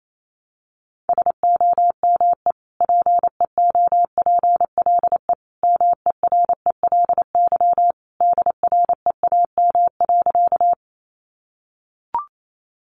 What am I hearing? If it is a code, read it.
Morse code, heard as a keyed tone, about 28 wpm, 700 Hz. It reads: SOME PEOPLE MERELY DREAM.
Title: Sentences from Top 1500 Words 28wpm